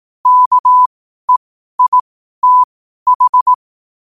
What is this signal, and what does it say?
Morse code, heard as a keyed tone, about 18 wpm, spaced as for 12 wpm, 1000 Hz. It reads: KEITH
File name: qotdfiles / 12wpm